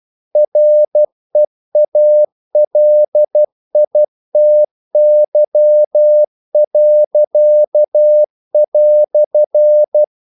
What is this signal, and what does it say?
Morse code, heard as a keyed tone, about 12 wpm, 600 Hz. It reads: REALITY."
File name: CW / Practice / EnglishQuotations0112